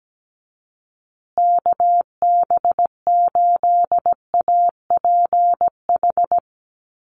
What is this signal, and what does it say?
Morse code, heard as a keyed tone, about 17 wpm, 700 Hz. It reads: KB8APH
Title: USA Call Signs Repeated 3x 17wpm